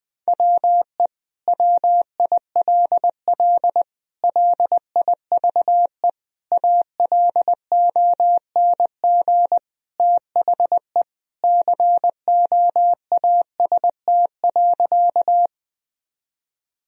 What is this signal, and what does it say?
Morse code, heard as a keyed tone, about 20 wpm, 700 Hz. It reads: WE WILL LIVE ALONG THE COAST.
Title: Sentences from Top 1000 Words - Review Part 4 - 20wpm